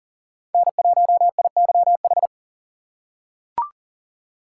Morse code, heard as a keyed tone, about 40 wpm, 700 Hz. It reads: N1IYH
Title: USA Call Signs - 5 Characters - 4x Character Spacing - 40wpm